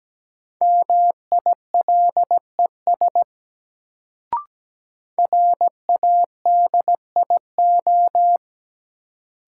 Morse code, heard as a keyed tone, about 17 wpm, 700 Hz. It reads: MILES RADIO
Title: Sentences from Top 1000 Words - Review Part 3 - New Word Drill - 17wpm